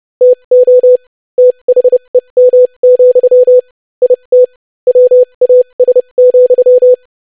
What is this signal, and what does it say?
Morse code, heard as a keyed tone, about 30 wpm, spaced as for 25 wpm, 500 Hz. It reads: TO THEM, IT WAS,
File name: lesson / basicphrase-25wpm